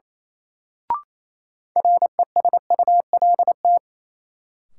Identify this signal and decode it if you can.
Morse code, heard as a keyed tone, about 28 wpm, 700 Hz. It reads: RESULT
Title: Sets of 3 Words - New Word Drill - 28wpm